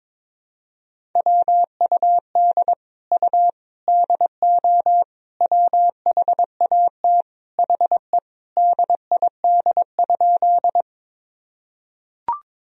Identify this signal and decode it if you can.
Morse code, heard as a keyed tone, about 22 wpm, 700 Hz. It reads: WUD U DO WHAT HE DID?
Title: Sentences from Top 100 Words and Abbreviations 22wpm